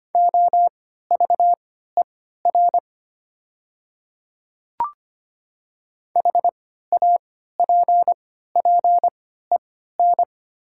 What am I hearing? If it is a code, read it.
Morse code, heard as a keyed tone, about 25 wpm, spaced as for 14 wpm, 700 Hz. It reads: OVER HAPPEN
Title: 3x Character Spacing - Top 500 Words - 25wpm